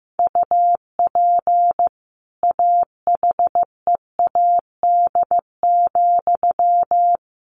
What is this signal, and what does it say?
Morse code, heard as a keyed tone, about 15 wpm, 700 Hz. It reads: UP AHEAD,
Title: A Sound Of Thunder 15wpm